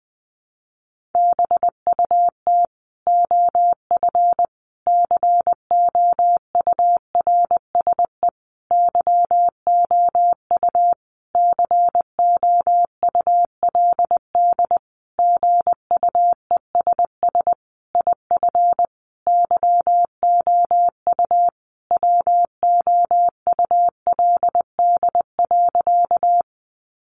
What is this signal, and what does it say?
Morse code, heard as a keyed tone, about 20 wpm, 700 Hz. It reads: BUT OF COURSE YOU COULD GUESS IF YOU WOULD.
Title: Sentences from Top 1500 Words 20wpm